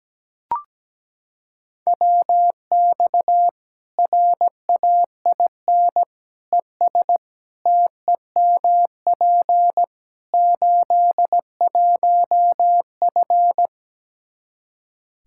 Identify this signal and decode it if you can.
Morse code, heard as a keyed tone, about 17 wpm, 700 Hz. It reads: WX RAIN ES TEMP 81F
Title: Practice QSO - Exchange 2 - Info - 17wpm